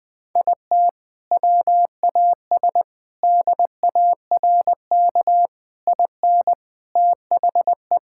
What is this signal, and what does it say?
Morse code, heard as a keyed tone, about 21 wpm, 700 Hz. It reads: IT WAS DARK IN THE
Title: Sentences from Top 1600 Words 20wpm